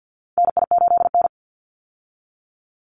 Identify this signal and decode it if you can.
Morse code, heard as a keyed tone, about 50 wpm, 700 Hz. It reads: NI8N
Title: Call Signs Repeated 2x - 50wpm